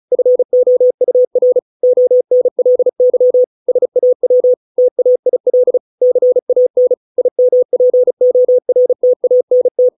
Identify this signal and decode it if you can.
Morse code, heard as a keyed tone, about 36 wpm, 500 Hz. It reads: FOUR ONLY SAW TAIL CAN IMPORTANT
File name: 632 words / 632 Words 35WPM